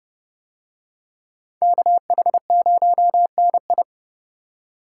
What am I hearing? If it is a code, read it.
Morse code, heard as a keyed tone, about 30 wpm, 700 Hz. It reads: KH0NI